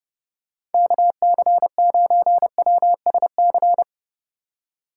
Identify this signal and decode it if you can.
Morse code, heard as a keyed tone, about 30 wpm, 700 Hz. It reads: KC9WSC